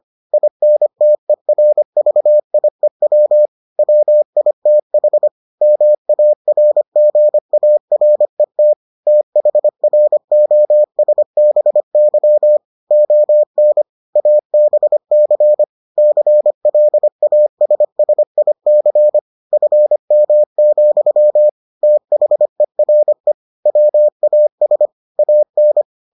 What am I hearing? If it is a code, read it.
Morse code, heard as a keyed tone, about 25 wpm, 600 Hz. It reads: INTERVIEW WITH MARGARET THROSBY ON ABC CLASSIC FM, THERE WAS AN